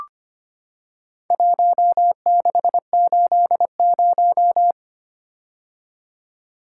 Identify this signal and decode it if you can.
Morse code, heard as a keyed tone, about 25 wpm, 700 Hz. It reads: 1680
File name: practice / 25wpm / Four Letter-Number Combinations - 25wpm